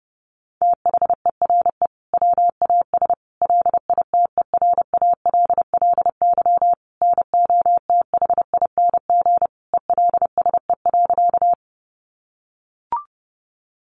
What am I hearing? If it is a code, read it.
Morse code, heard as a keyed tone, about 30 wpm, 700 Hz. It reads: THERE WAS LITERALLY NOTHING ELSE.
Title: Sentences from Top 1400 Words 30wpm